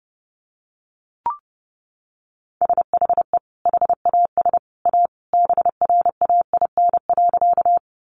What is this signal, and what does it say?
Morse code, heard as a keyed tone, about 30 wpm, 700 Hz. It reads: SHE HAS A BRAIN.
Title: Sentences from Top 1100 Words 30wpm